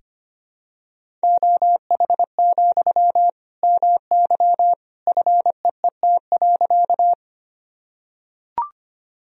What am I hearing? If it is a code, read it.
Morse code, heard as a keyed tone, about 25 wpm, 700 Hz. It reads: OH, MY FEET.
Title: Sentences from Top 1000 Words - Review Part 1 - 25wpm